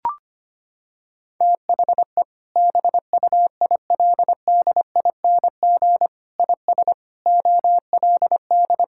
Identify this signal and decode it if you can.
Morse code, heard as a keyed tone, about 25 wpm, 700 Hz. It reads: THE BUILDING IS OLD
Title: Sentences from Top 1000 Words - Review Part 5 - 25wpm